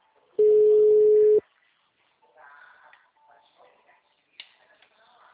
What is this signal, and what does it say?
Ringback tone (Europe, 425 Hz).